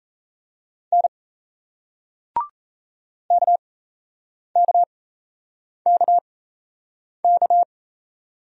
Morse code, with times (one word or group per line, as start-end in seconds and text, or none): 0.9-1.1 N
3.3-7.6 KKKK